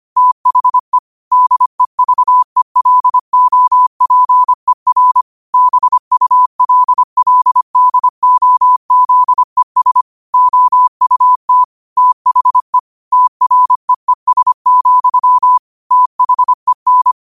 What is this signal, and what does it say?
Morse code, heard as a keyed tone, about 25 wpm, 1000 Hz. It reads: THE DEVELOPER BULLDOZES OUT THE TREES, THEN
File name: qotdfiles / 25wpm